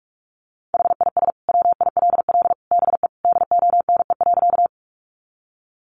Morse code, heard as a keyed tone, about 44 wpm, 700 Hz. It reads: HIS WILL BE DONE.
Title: Sentences from Top 500 Words 45wpm